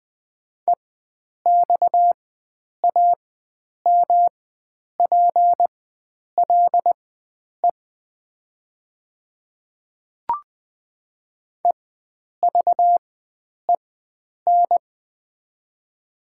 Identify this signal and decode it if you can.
Morse code, heard as a keyed tone, about 20 wpm, spaced as for 9 wpm, 700 Hz. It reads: EXAMPLE EVEN
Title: 4x Character Spacing - Top 300 Words - 20wpm